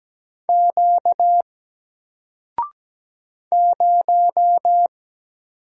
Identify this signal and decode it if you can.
Morse code, heard as a keyed tone, about 17 wpm, 700 Hz. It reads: Q 0
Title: Single Letter-Number 17wpm